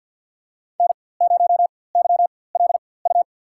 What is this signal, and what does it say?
Morse code, heard as a keyed tone, about 53 wpm, spaced as for 24 wpm, 700 Hz. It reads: N0YLU
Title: USA Call Signs - 5 Characters - 4x Character Spacing - 50wpm